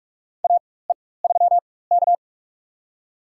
Morse code, heard as a keyed tone, about 42 wpm, spaced as for 21 wpm, 690 Hz. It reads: AE3K
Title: USA Call Signs - 4x Character Spacing - 45wpm